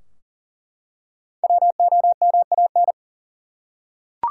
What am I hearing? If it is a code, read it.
Morse code, heard as a keyed tone, about 39 wpm, 700 Hz. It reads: WOMAN
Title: Five Letter Words - 40wpm